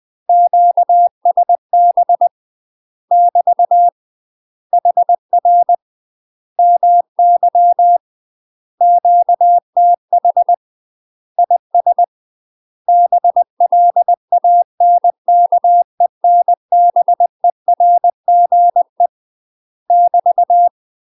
Morse code, heard as a keyed tone, about 20 wpm, 700 Hz. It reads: QSB <BT> HR MY QTH IS BLANKENBERGE <BT>
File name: UK MASTER QSOs TRAINING / 20WPM / UK MASTER QSOs 0010